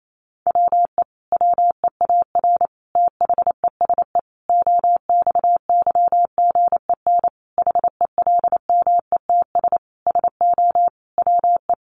Morse code, heard as a keyed tone, about 28 wpm, 700 Hz. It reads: WE WEAR THESE OXYGEN HELMETS SO WE
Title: A Sound Of Thunder 28wpm